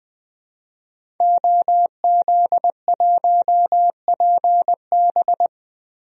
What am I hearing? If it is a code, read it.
Morse code, heard as a keyed tone, about 20 wpm, 700 Hz. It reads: OZ1PB
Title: DXCC-100 Call Signs - 5 Characters - 20wpm